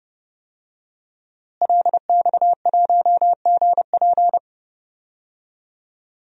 Morse code, heard as a keyed tone, about 30 wpm, 700 Hz. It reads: LX1GP
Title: DXCC-100 Call Signs - 5 Characters - Repeated 2x - 30wpm